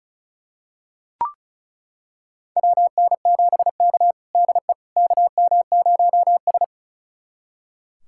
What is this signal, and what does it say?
Morse code, heard as a keyed tone, about 36 wpm, 700 Hz. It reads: WN7K DE KM0S
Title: USA Callsign DE Callsign - 4 Characters - 35wpm